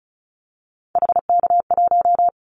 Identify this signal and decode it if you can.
Morse code, heard as a keyed tone, about 34 wpm, 700 Hz. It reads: HK1
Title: Callsign Prefixes for Top 100 DXCC - 35wpm